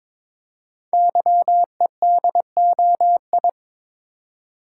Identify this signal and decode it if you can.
Morse code, heard as a keyed tone, about 22 wpm, 700 Hz. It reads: YEDOI
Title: Code Groups - 5 Letters - 22wpm